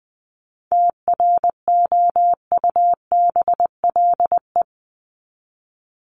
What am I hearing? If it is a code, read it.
Morse code, heard as a keyed tone, about 20 wpm, 700 Hz. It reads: TROUBLE
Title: Top 1001-1250 Words 20wpm